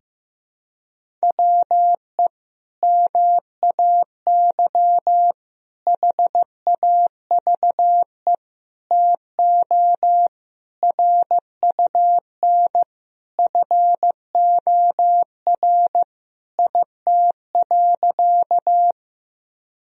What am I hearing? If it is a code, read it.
Morse code, heard as a keyed tone, about 15 wpm, 700 Hz. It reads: WE MAY HAVE TO RUN FOR IT.
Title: Sentences from Top 1000 Words - Review Part 1 - 15wpm